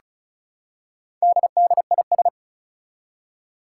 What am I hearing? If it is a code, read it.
Morse code, heard as a keyed tone, about 36 wpm, 700 Hz. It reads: DDIS
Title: Four Challenging Letter-Number Combinations 35wpm